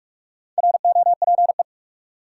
Morse code, heard as a keyed tone, about 46 wpm, 700 Hz. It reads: ROPE
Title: Four Letter Words - 45wpm